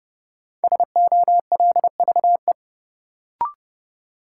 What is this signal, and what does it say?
Morse code, heard as a keyed tone, about 30 wpm, 700 Hz. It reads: SOLVE
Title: Sentences from Top 1000 Words - New Word Drill - 30wpm